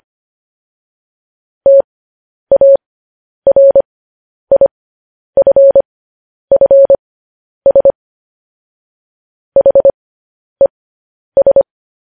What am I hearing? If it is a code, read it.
Morse code, heard as a keyed tone, about 25 wpm, spaced as for 10 wpm, 550 Hz. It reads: TARIFFS HES
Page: News Headlines in Morse Code at 10 WPM